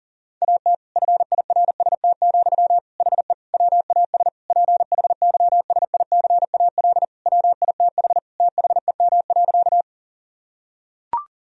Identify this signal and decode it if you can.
Morse code, heard as a keyed tone, about 37 wpm, 700 Hz. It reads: AT FIRST, HE WAS PHYSICAL WITH THEM.
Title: Sentences from Top 1000 Words - Review Part 4 - 40wpm